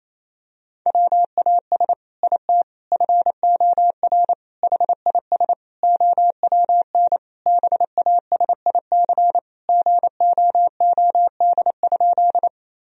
Morse code, heard as a keyed tone, about 28 wpm, 700 Hz. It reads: WAS IT FOR HIS OWN BASIC GOOD?
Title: Sentences from Top 1000 Words - Review Part 3 - 28wpm